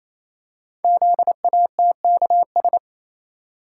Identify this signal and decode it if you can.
Morse code, heard as a keyed tone, about 27 wpm, 700 Hz. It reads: ZATKS
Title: Code Groups - 5 Letters - 28wpm